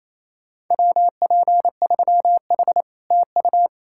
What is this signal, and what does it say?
Morse code, heard as a keyed tone, about 38 wpm, spaced as for 33 wpm, 700 Hz. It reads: WP3H TU
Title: Speed-Racing Common QSO Phrases - 28wpm